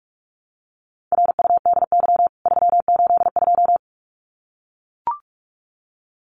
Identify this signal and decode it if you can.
Morse code, heard as a keyed tone, about 44 wpm, 700 Hz. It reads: RUDY 382
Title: CWT Exchange 45wpm